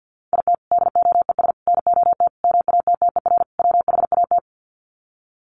Morse code, heard as a keyed tone, about 50 wpm, 700 Hz. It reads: IT DOES NOT MATTER WHAT
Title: Sets of 5 Words Encore 50wpm